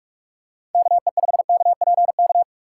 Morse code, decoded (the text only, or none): KE5KPK